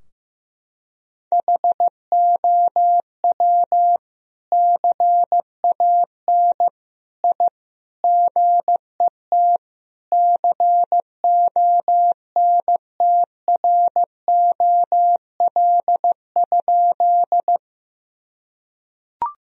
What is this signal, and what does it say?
Morse code, heard as a keyed tone, about 15 wpm, 700 Hz. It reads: HOW CAN I GET CONTROL?
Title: Sentences from Top 1000 Words - Review Part 3 - 15wpm